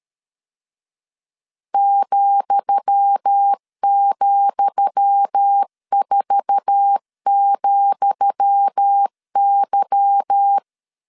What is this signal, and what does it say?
Morse code, heard as a keyed tone, about 13 wpm, 800 Hz. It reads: ,,4,Y
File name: LEZIONI / 10 SIGNS / 10D COMA 12wpm 2544